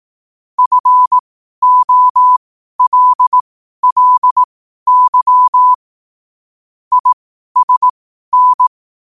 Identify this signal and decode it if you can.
Morse code, heard as a keyed tone, about 18 wpm, spaced as for 12 wpm, 1000 Hz. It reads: FOLLY ISN
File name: 12wpm